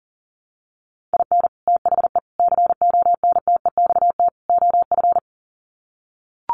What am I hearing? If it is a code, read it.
Morse code, heard as a keyed tone, about 40 wpm, 700 Hz. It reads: IN THE CONTEXT OF
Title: Sets of 4 Words 40wpm